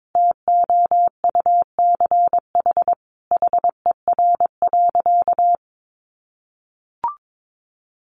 Morse code, heard as a keyed tone, about 22 wpm, 700 Hz. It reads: TOUCH HER.
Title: Sentences from Top 1200 Words 22wpm